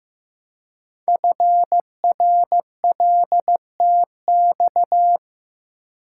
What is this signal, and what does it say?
Morse code, heard as a keyed tone, about 15 wpm, 700 Hz. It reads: FRLTX
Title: Code Groups - 5 Letters - 15wpm